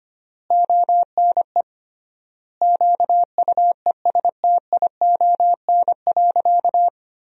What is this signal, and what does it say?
Morse code, heard as a keyed tone, about 25 wpm, 700 Hz. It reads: ONE QUESTION.
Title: 3x Word-Spacing - Sentences from Top 400 Words - 25wpm